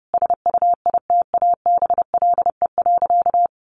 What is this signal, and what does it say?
Morse code, heard as a keyed tone, about 30 wpm, 700 Hz. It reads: SUITABLE.
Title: Sentences from Top 1400 Words 30wpm